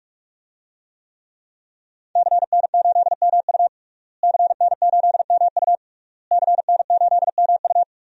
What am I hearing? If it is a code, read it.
Morse code, heard as a keyed tone, about 46 wpm, 700 Hz. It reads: CN8MU CN8MU CN8MU
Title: DXCC-100 Call Signs - 5 Characters - Repeated 3x - 45wpm